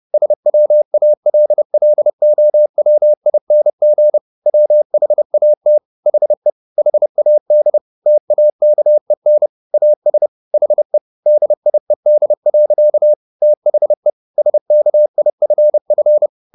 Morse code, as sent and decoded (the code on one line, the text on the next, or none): ... .-- .- .-.. .-.. --- .-- .. -. --. / .-- .... .- - / .... / .... .- -.. / - .- -.- . -. / .- ... / .... / -.. .. -.. .-.-.- / - .... / ... -.- .. ..-. ..-.
SWALLOWING WHAT HE HAD TAKEN AS HE DIED. THE SKIFF